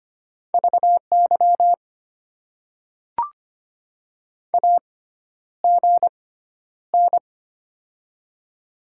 Morse code, heard as a keyed tone, about 25 wpm, spaced as for 11 wpm, 700 Hz. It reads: VY AGN
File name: Top 90 QSO Elements - 6x Character Spacing - 25wpm